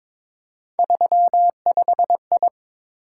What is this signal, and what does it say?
Morse code, heard as a keyed tone, about 22 wpm, 700 Hz. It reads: 35I
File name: practice / 22wpm / Three Letter-Number Combinations - 22wpm